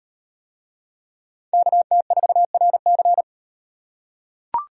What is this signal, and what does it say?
Morse code, heard as a keyed tone, about 36 wpm, 700 Hz. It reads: KT4RC